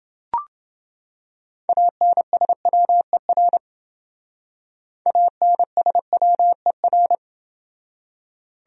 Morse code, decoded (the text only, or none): ANSWER ANSWER